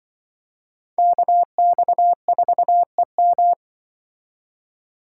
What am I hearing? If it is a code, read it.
Morse code, heard as a keyed tone, about 25 wpm, 700 Hz. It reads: KX4EM